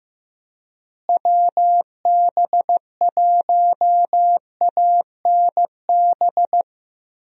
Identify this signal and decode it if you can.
Morse code, heard as a keyed tone, about 15 wpm, 700 Hz. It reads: WB1ANB
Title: USA Call Signs - 6 Characters - Repeated 3x - 15wpm